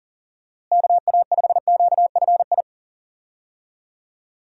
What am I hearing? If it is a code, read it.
Morse code, heard as a keyed tone, about 40 wpm, 700 Hz. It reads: KA5QFI
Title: USA Call Signs - 6 Characters - Repeated 3x - 40wpm